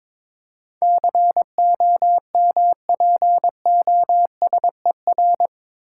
Morse code, heard as a keyed tone, about 22 wpm, 700 Hz. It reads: COMPOSER
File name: practice / 22wpm / Eight Letter Words - 22wpm